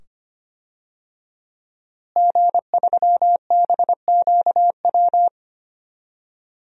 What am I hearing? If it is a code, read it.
Morse code, heard as a keyed tone, about 25 wpm, 700 Hz. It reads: G3BQW
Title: DXCC-100 Call Signs - 5 Characters - Repeated 2x - 25wpm